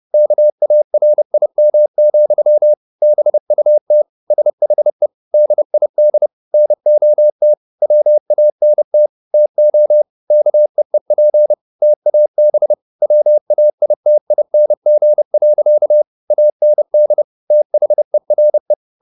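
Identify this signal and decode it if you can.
Morse code, heard as a keyed tone, about 29 wpm, 600 Hz. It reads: KARIM, BUT SHE DID NOT WANT TO KEEP TAB WAITING. AND THERE
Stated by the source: Morse code at 30 wpm, the novel Never by Ken Follett in English, from CW med Gnister.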